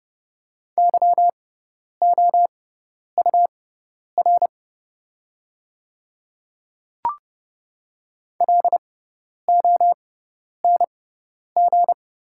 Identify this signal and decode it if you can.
Morse code, heard as a keyed tone, about 31 wpm, spaced as for 10 wpm, 700 Hz. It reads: YOUR LONG
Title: 6x Character Spacing - Top 100 Words - 30wpm